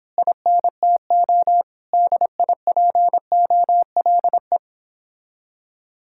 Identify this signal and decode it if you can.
Morse code, heard as a keyed tone, about 25 wpm, 700 Hz. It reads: INTO DIPOLE
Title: Speed-Racing Common QSO Phrases - 22wpm